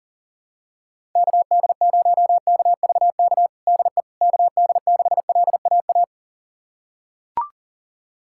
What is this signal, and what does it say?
Morse code, heard as a keyed tone, about 40 wpm, 700 Hz. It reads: KD0KVK DE KD6LAA
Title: USA Callsign DE Callsign - 6 Characters - 40wpm